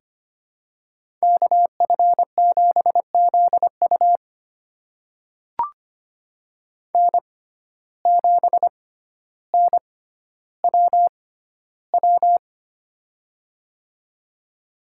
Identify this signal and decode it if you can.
Morse code, heard as a keyed tone, about 25 wpm, spaced as for 13 wpm, 700 Hz. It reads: KF7ZU N7NWW
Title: USA Call Signs - 5 Characters - 6x Character Spacing - 25wpm